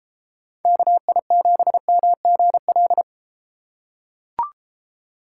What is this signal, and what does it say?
Morse code, heard as a keyed tone, about 32 wpm, 700 Hz. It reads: KI7MGL